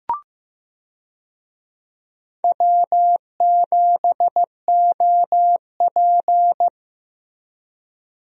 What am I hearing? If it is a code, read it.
Morse code, heard as a keyed tone, about 15 wpm, 700 Hz. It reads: W7OP
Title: USA Call Signs - 4 Characters - Repeated 2x - 15wpm